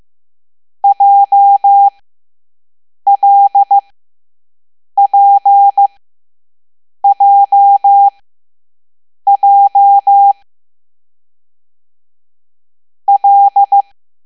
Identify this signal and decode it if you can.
Morse code, heard as a keyed tone, about 15 wpm, spaced as for 6 wpm, 800 Hz. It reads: JLPJJ L
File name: Morse / Level7A